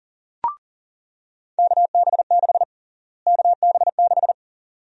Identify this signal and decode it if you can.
Morse code, heard as a keyed tone, about 40 wpm, 700 Hz. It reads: KB6 KB6
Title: USA Call Sign Prefixes - Repeated 2x - 40wpm